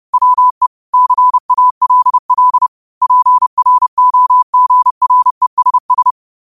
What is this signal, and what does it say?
Morse code, heard as a keyed tone, about 30 wpm, 1000 Hz. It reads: WE CALL PROGRESS
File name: qotdfiles / 30wpm